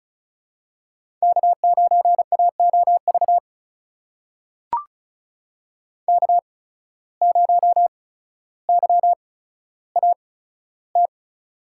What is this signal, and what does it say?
Morse code, heard as a keyed tone, about 34 wpm, spaced as for 15 wpm, 700 Hz. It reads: K9AOV K0YAT
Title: USA Call Signs - 5 Characters - 8x Character Spacing - 35wpm